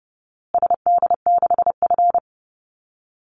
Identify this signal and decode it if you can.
Morse code, heard as a keyed tone, about 28 wpm, 700 Hz. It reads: SD6F
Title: Call Signs Repeated 2x - 30wpm